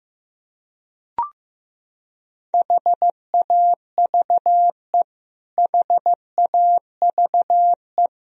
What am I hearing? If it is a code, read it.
Morse code, heard as a keyed tone, about 15 wpm, 700 Hz. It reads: HAVE HAVE
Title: Top 100 Words Repeated 15wpm